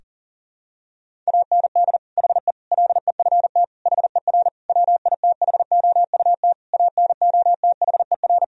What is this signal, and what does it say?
Morse code, heard as a keyed tone, about 40 wpm, 700 Hz. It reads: AND HE LEFT HER WITHOUT ANOTHER
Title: Sentences from Top 1000 Words - Review Part 3 - 40wpm